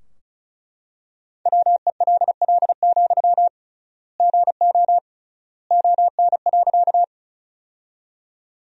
Morse code, heard as a keyed tone, about 34 wpm, 700 Hz. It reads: WELL, GO ON.
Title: 3x Word-Spacing - Sentences from Top 200 Words - 35wpm